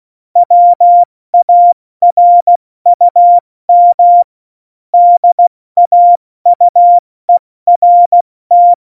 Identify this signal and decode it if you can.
Morse code, heard as a keyed tone, about 16 wpm, 700 Hz. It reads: WARUM DAUERT
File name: Morsekurs / Witze / Witz 13 March 2018, 70 F80